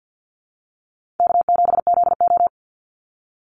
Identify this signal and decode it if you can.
Morse code, heard as a keyed tone, about 50 wpm, 700 Hz. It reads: K7ZO
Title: NA SOTA Chaser Callsigns 50wpm